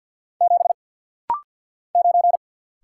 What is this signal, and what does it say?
Morse code, heard as a keyed tone, about 50 wpm, 700 Hz. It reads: 7 9
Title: Single Letter-Number - Rapid-Fire 50wpm